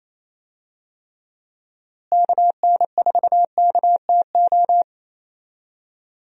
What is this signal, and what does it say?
Morse code, heard as a keyed tone, about 28 wpm, 700 Hz. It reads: KN4KTO
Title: USA Call Signs - 6 Characters - Repeated 3x - 28wpm